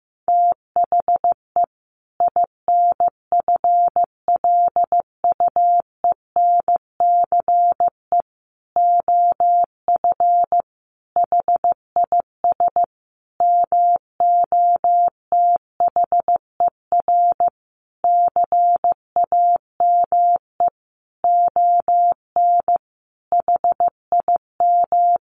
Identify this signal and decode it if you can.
Morse code, heard as a keyed tone, about 15 wpm, 700 Hz. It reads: THE INFLUENCE OF HIS MOTHER CAME ON HIM
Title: Sentences from Top 1100 Words 15wpm